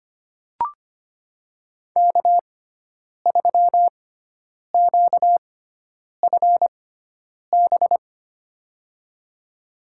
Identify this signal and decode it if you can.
Morse code, heard as a keyed tone, about 25 wpm, spaced as for 9 wpm, 700 Hz. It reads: K3QFB
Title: USA Call Signs - 5 Characters - 6x Character Spacing - 25wpm